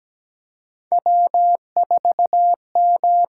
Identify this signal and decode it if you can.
Morse code, heard as a keyed tone, about 17 wpm, 700 Hz. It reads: W4M
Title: USA Call Signs - 4 Characters - 8x Character Spacing - 17wpm